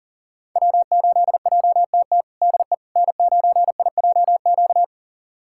Morse code, heard as a keyed tone, about 40 wpm, 700 Hz. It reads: W8JTT DE N9IJQ
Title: USA Callsign DE Callsign - 5 Characters - 40wpm